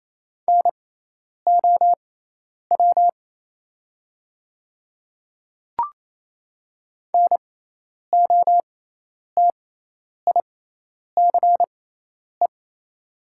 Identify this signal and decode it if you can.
Morse code, heard as a keyed tone, about 28 wpm, spaced as for 10 wpm, 690 Hz. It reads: NOW NOTICE
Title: 6x Character Spacing - Top 500 Words - 28wpm